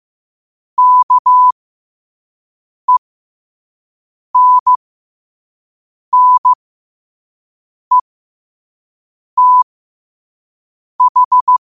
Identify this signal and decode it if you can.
Morse code, heard as a keyed tone, about 15 wpm, spaced as for 5 wpm, 1000 Hz. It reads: KENNETH
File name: qotdfiles / 5wpm